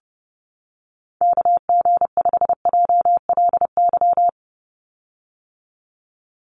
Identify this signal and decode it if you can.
Morse code, heard as a keyed tone, about 30 wpm, 700 Hz. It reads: KG5JLY
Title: USA Call Signs 30wpm